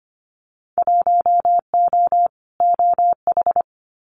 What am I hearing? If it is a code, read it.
Morse code, heard as a keyed tone, about 25 wpm, 700 Hz. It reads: 1O OH
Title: Winter Field Day Exchanges 25wpm